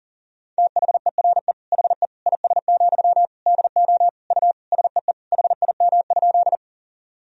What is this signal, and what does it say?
Morse code, heard as a keyed tone, about 40 wpm, 700 Hz. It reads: THERE HE IS, DO U SEE HIM?
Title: Sentences from Top 100 Words and Abbreviations 40wpm